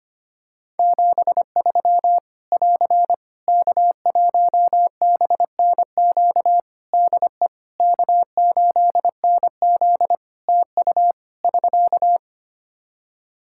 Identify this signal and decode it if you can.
Morse code, heard as a keyed tone, about 25 wpm, 700 Hz. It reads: 73 <AR> K1BNQ DE K8NZ TU <SK>
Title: Practice QSO - Exchange 4 - Ending - 25wpm